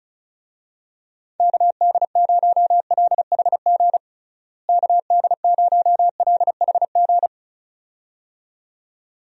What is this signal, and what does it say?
Morse code, heard as a keyed tone, about 34 wpm, 700 Hz. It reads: KD0LHG KD0LHG
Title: USA Call Signs - 6 Characters - Repeated 2x - 35wpm